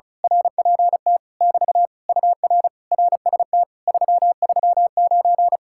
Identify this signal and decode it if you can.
Morse code, heard as a keyed tone, about 33 wpm, 700 Hz. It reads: RPT <BT> UR RST 339
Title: Speed-Racing Common QSO Phrases Encore - 35wpm